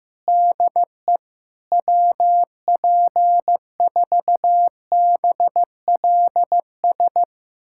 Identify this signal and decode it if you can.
Morse code, heard as a keyed tone, about 16 wpm, 700 Hz. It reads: DE WP4BLS
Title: USA Callsign DE Callsign - 6 Characters - 15wpm